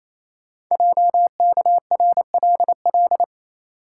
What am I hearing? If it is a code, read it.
Morse code, heard as a keyed tone, about 27 wpm, 700 Hz. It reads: JKRLL
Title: Code Groups - 5 Letters - 28wpm